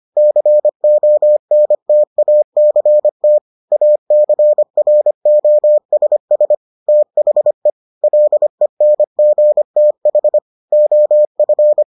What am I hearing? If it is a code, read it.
Morse code, heard as a keyed tone, about 25 wpm, 600 Hz. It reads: CONTACT ACROSS THE LENGTH OF